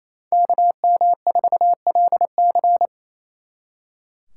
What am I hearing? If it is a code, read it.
Morse code, heard as a keyed tone, about 37 wpm, 700 Hz. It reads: KM4LC